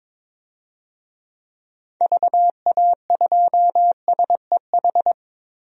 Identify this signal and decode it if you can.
Morse code, heard as a keyed tone, about 22 wpm, 700 Hz. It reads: VA2SEH
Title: DXCC-100 Call Signs - 6 Characters - 22wpm